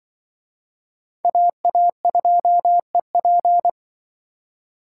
Morse code, heard as a keyed tone, about 25 wpm, 700 Hz. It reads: AA2EP
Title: Speed Racing - USA Call Signs 20wpm